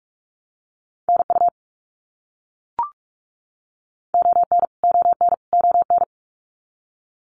Morse code, heard as a keyed tone, about 44 wpm, 700 Hz. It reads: NU ON ON ON
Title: Canadian Provinces and Territories - Repeated 3x - 45wpm